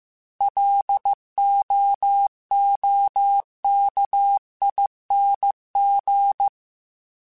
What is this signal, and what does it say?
Morse code, heard as a keyed tone, about 15 wpm, 800 Hz. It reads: LOOKING
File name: Alphabetical / Alphabetical mp3s / Ls @13